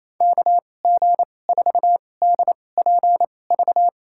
Morse code, heard as a keyed tone, about 27 wpm, spaced as for 20 wpm, 700 Hz. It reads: KG4DPV
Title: USA Call Signs - 6 Characters - 2x Character Spacing - 28wpm